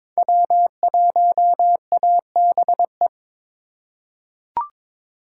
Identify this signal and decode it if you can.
Morse code, heard as a keyed tone, about 22 wpm, 700 Hz. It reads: W1ABE
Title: USA Call Signs - 5 Characters - 4x Character Spacing - 22wpm